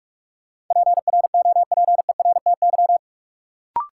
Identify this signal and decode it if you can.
Morse code, heard as a keyed tone, about 46 wpm, 700 Hz. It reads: PROPERTY
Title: Sentences from Top 1000 Words - New Word Drill - 45wpm